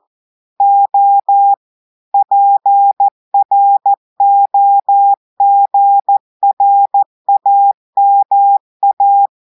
Morse code, heard as a keyed tone, about 14 wpm, 800 Hz. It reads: O PROGRAMA
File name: audios / portugais / portugais texte en clair 2 vitesse 14 mots minutes